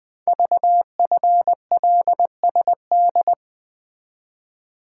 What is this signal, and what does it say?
Morse code, heard as a keyed tone, about 20 wpm, 700 Hz. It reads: VFLSD
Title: Code Groups - 5 Letters - 20wpm